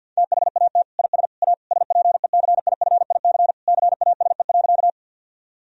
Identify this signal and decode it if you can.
Morse code, heard as a keyed tone, about 50 wpm, 700 Hz. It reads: THAT IS A SPECIFIC CASE.
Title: Sentences from Top 1000 Words - Review Part 3 - 50wpm